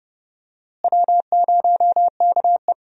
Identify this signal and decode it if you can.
Morse code, heard as a keyed tone, about 30 wpm, 700 Hz. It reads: W0KE
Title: USA Call Signs - 4 Characters - Repeated 2x - 30wpm